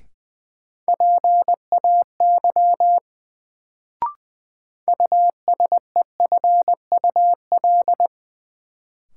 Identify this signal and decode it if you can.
Morse code, heard as a keyed tone, about 20 wpm, 700 Hz. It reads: PAY USEFUL
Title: Sentences from Top 700 Words - New Word Drill - 20wpm